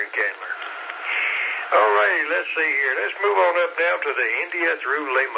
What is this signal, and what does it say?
Emergency Alert System (SAME) tones, heard as harsh buzzing bursts, with speech. It is heard from 1.0 to 4.8 s.